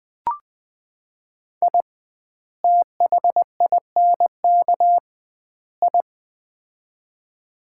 Morse code, heard as a keyed tone, about 20 wpm, 700 Hz. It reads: I THINK I
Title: 2x Word-Spacing - Sets of 3 Words - 20wpm